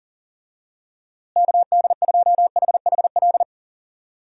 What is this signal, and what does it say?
Morse code, heard as a keyed tone, about 40 wpm, 700 Hz. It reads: KD2HHL
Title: USA Call Signs - 6 Characters - Repeated 3x - 40wpm